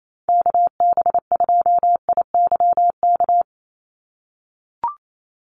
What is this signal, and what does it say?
Morse code, heard as a keyed tone, about 28 wpm, 700 Hz. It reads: KB2IYK
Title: USA Call Signs 28wpm